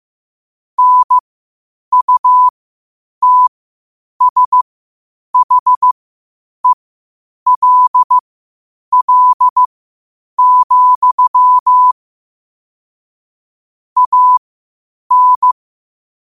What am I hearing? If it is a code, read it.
Morse code, heard as a keyed tone, about 15 wpm, spaced as for 8 wpm, 1000 Hz. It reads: NUTSHELL, AN
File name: qotdfiles / 8wpm